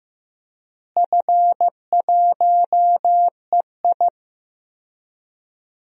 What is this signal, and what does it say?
Morse code, heard as a keyed tone, about 15 wpm, 700 Hz. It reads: F1EI
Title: DXCC-100 Call Signs - 4 Characters - Repeated 2x - 15wpm